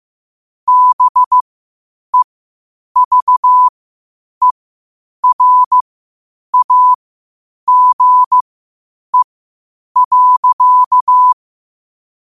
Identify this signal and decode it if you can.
Morse code, heard as a keyed tone, about 15 wpm, spaced as for 8 wpm, 1000 Hz. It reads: BEVERAGE.
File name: qotdfiles / 8wpm